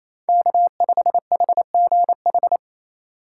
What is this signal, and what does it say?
Morse code, heard as a keyed tone, about 34 wpm, 700 Hz. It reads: K5HGH
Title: Speed Racing - USA Call Signs 28wpm